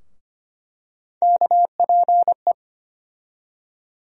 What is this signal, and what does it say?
Morse code, heard as a keyed tone, about 25 wpm, 700 Hz. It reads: KPE
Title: USA Call Sign Postfixes - 3 Letters 25wpm